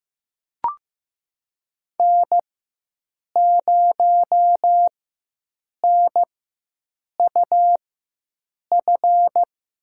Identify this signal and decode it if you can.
Morse code, heard as a keyed tone, about 15 wpm, spaced as for 7 wpm, 700 Hz. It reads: N0NUF
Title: USA Call Signs - 5 Characters - 4x Character Spacing - 15wpm